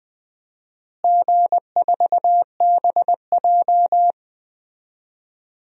Morse code, heard as a keyed tone, about 20 wpm, 700 Hz. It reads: G4BJ
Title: DXCC-100 Call Signs - 4 Characters - Repeated 2x - 20wpm